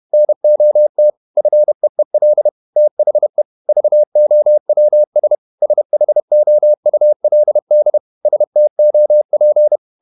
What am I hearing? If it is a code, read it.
Morse code, heard as a keyed tone, about 31 wpm, 600 Hz. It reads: NOT FEEL THE VOWS SHOULD STOP
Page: Never 31wpm | CW med Gnister